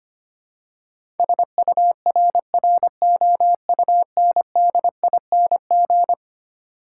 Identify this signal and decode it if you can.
Morse code, heard as a keyed tone, about 25 wpm, 700 Hz. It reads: SURROUNDING
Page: morse-code-ninja-practice-25wpm